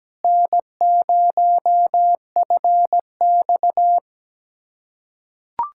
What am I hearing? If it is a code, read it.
Morse code, heard as a keyed tone, about 17 wpm, 700 Hz. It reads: N0FX
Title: USA Callsign DE Callsign - 4 Characters - 17wpm